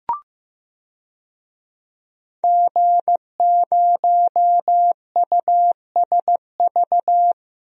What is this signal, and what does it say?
Morse code, heard as a keyed tone, about 15 wpm, 700 Hz. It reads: G0USV
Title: DXCC-100 Call Signs - 5 Characters - 15wpm